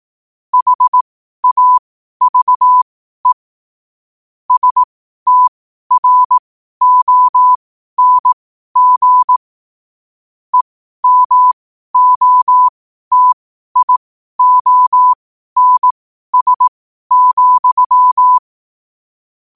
Morse code, heard as a keyed tone, about 18 wpm, spaced as for 12 wpm, 1000 Hz. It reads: HAVE STRONG EMOTIONS,
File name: qotdfiles / 12wpm